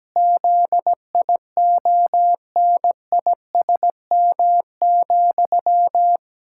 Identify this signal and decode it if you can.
Morse code, heard as a keyed tone, about 17 wpm, 700 Hz. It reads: ZIONISM,
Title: Pangram Challenge v2 - 17wpm